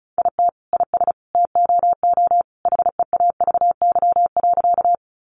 Morse code, heard as a keyed tone, about 34 wpm, 700 Hz. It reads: IT IS TOO HEAVY.
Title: Sentences from Top 1100 Words 35wpm